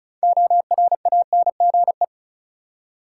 Morse code, heard as a keyed tone, about 34 wpm, 700 Hz. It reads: ORANGE
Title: Six Letter Words - 35wpm